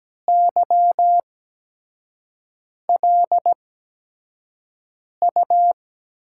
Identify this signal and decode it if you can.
Morse code, heard as a keyed tone, about 17 wpm, spaced as for 5 wpm, 700 Hz. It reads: YLU
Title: USA Call Signs - 5 Characters - 8x Character Spacing - 17wpm